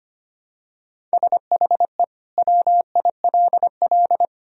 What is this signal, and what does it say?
Morse code, heard as a keyed tone, about 25 wpm, 700 Hz. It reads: SHE WILL
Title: Sentences from Top 1000 Words - Review Part 1 - 25wpm